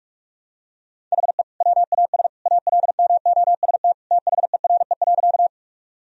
Morse code, heard as a keyed tone, about 46 wpm, 700 Hz. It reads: HE WAS ALMOST THERE.
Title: Sentences from Top 1000 Words - Review Part 4 - 45wpm